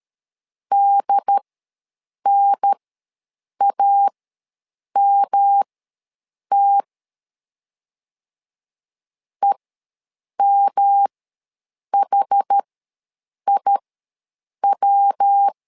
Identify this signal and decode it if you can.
Morse code, heard as a keyed tone, about 13 wpm, spaced as for 7 wpm, 800 Hz. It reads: DNAMT EMHIW
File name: LEZIONI / 3 MWRO / 3 MWR ws 12wpm 1809